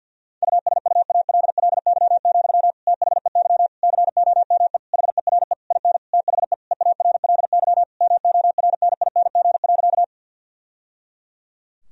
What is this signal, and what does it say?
Morse code, heard as a keyed tone, about 50 wpm, 700 Hz. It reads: USUALLY, THEY COME HERE IN THE EARLY MORNING.
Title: Sentences from Top 1000 Words - Review Part 3 - 50wpm